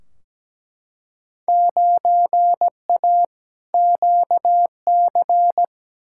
Morse code, heard as a keyed tone, about 17 wpm, 700 Hz. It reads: 9A QC
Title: Field Day Exchanges Encore - 17wpm